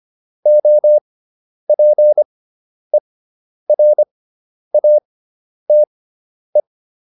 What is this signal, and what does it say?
Morse code, heard as a keyed tone, about 25 wpm, spaced as for 10 wpm, 600 Hz. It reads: OPERATE